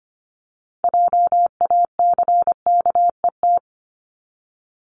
Morse code, heard as a keyed tone, about 25 wpm, 700 Hz. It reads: JACKET
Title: Top 1751-2000 Words 25wpm